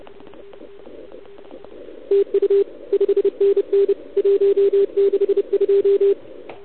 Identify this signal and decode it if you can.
Morse code, heard as a keyed tone, about 29 wpm, 410 Hz. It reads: TU 5NN 162
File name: ARCHIV / OKDX2019 / OKOM40 KX3